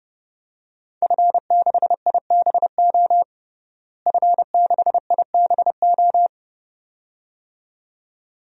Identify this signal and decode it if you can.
Morse code, heard as a keyed tone, about 30 wpm, 700 Hz. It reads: F6IBO F6IBO
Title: DXCC-100 Call Signs - 5 Characters - Repeated 2x - 30wpm